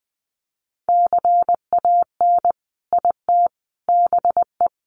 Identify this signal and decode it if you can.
Morse code, heard as a keyed tone, about 20 wpm, 700 Hz. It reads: CAN IT BE
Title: Sentences from Top 500 Words 20wpm